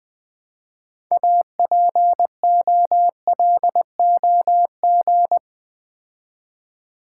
Morse code, heard as a keyed tone, about 20 wpm, 700 Hz. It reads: APOLOG
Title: Build A Word - 3 Letters At A Time - 9-Letter Words - 20wpm